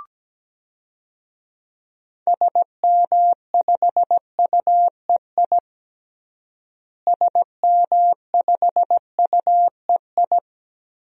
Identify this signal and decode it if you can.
Morse code, heard as a keyed tone, about 17 wpm, 700 Hz. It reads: SM5UEI SM5UEI
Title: DXCC-100 Call Signs - 6 Characters - Repeated 2x - 17wpm